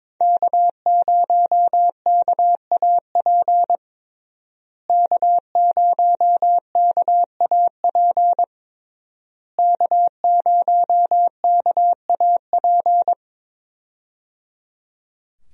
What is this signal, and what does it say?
Morse code, heard as a keyed tone, about 22 wpm, 700 Hz. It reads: K0KAP K0KAP K0KAP
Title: USA Call Signs - 5 Characters - Repeated 3x - 22wpm